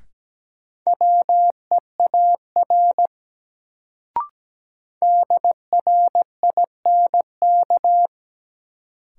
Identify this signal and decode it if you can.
Morse code, heard as a keyed tone, about 17 wpm, 700 Hz. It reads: WEAR DRINK
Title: Sentences from Top 1500 Words - New Word Drill Alt - 17wpm